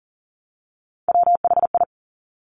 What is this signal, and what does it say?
Morse code, heard as a keyed tone, about 41 wpm, 690 Hz. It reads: WHI
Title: Top Three Letter Combinations in Words 40wpm